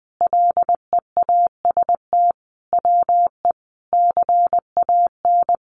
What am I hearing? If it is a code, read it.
Morse code, heard as a keyed tone, about 20 wpm, 700 Hz. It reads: LEAST WE CAN
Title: Sentences from Top 700 Words 20wpm